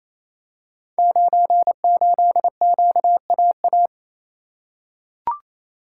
Morse code, heard as a keyed tone, about 28 wpm, 700 Hz. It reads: 98QAA
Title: Code Groups - 5 Letter:Numbers - 28wpm